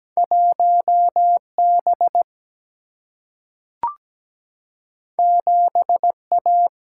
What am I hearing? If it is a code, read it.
Morse code, heard as a keyed tone, about 17 wpm, 700 Hz. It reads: 1 B 7 A
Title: Single-Letter Single-Number 17wpm